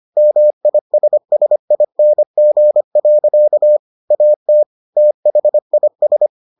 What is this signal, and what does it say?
Morse code, heard as a keyed tone, about 25 wpm, 600 Hz. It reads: MISSING. AT THIS